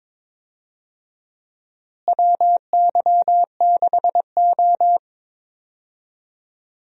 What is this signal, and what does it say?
Morse code, heard as a keyed tone, about 22 wpm, 700 Hz. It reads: WY6O
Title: USA Call Signs - 4 Characters - Repeated 2x - 22wpm